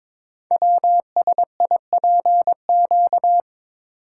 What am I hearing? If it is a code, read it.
Morse code, heard as a keyed tone, about 22 wpm, 700 Hz. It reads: WSIPQ
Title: Code Groups - 5 Letters - 22wpm